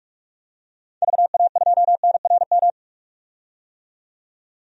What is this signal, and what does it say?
Morse code, heard as a keyed tone, about 46 wpm, 700 Hz. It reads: VA2NRM
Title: DXCC-100 Call Signs - 6 Characters - 45wpm